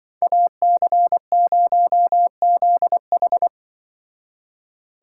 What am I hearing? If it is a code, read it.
Morse code, heard as a keyed tone, about 24 wpm, 700 Hz. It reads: AC0ZH
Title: Speed Racing - USA 2x2 Call Signs 20wpm